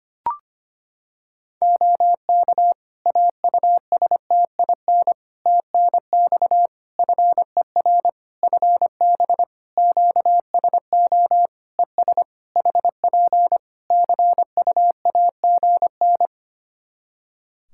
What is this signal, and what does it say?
Morse code, heard as a keyed tone, about 25 wpm, 700 Hz. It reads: OK AUSTIN TNX FER FB QSO ES HP CUAGN
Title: Practice QSO - Exchange 4 - Ending - 25wpm